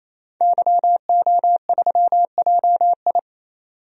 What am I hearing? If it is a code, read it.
Morse code, heard as a keyed tone, about 28 wpm, 700 Hz. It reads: YO3JI
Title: DXCC-100 Call Signs - 5 Characters - 28wpm